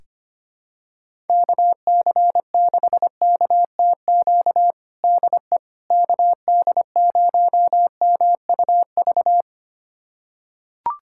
Morse code, heard as a keyed tone, about 25 wpm, 700 Hz. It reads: KC6KTQ DE KD0MUV
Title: USA Callsign DE Callsign 25wpm